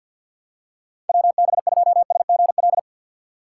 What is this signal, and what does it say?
Morse code, heard as a keyed tone, about 53 wpm, 700 Hz. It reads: WB2SGL
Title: USA Call Signs - 6 Characters - 4x Character Spacing - 50wpm